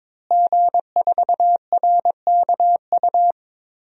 Morse code, heard as a keyed tone, about 22 wpm, 700 Hz. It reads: G4RKU
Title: DXCC-100 Call Signs - 5 Characters - Repeated 2x - 22wpm